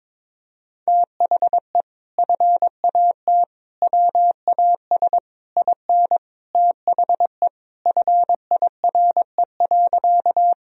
Morse code, heard as a keyed tone, about 21 wpm, 700 Hz. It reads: THE FAT WAS IN THE FIRE.
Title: Sentences from Top 1000 Words - Review Part 4 - 22wpm